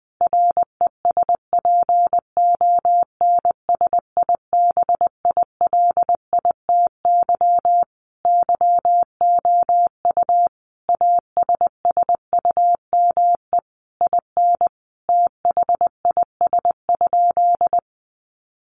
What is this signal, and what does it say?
Morse code, heard as a keyed tone, about 20 wpm, 700 Hz. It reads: RESPONSIBILITY YOU ASSUME IN THIS?
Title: Sentences from Top 1500 Words 20wpm